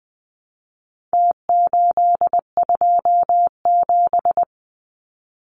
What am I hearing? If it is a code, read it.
Morse code, heard as a keyed tone, about 20 wpm, 700 Hz. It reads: T827
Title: Numbers - Contest Style - 4-Digit 20wpm